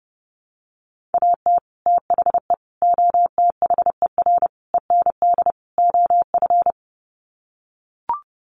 Morse code, heard as a keyed tone, about 30 wpm, 700 Hz. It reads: AT THE OTHER END OF
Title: Sets of 5 Words 30wpm